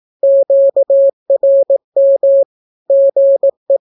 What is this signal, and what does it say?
Morse code, heard as a keyed tone, about 18 wpm, 550 Hz. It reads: QRM GE
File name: naukapodstaw / skroty / skroty